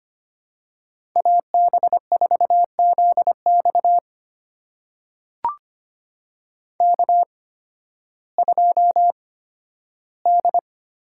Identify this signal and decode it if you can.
Morse code, heard as a keyed tone, about 25 wpm, spaced as for 13 wpm, 700 Hz. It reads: AB4ZX K2D
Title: USA Call Signs - 5 Characters - 8x Character Spacing - 25wpm